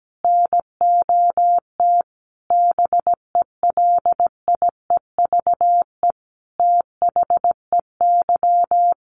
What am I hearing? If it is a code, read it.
Morse code, heard as a keyed tone, about 17 wpm, 700 Hz. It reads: NOT BELIEVE THEY
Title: Sentences from Top 1000 Words 17wpm